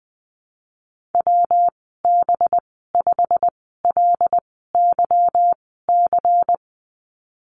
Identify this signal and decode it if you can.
Morse code, heard as a keyed tone, about 20 wpm, spaced as for 15 wpm, 700 Hz. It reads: WB5LYC
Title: 193 Lesson 31g - Letter v - Call Signs 20wpm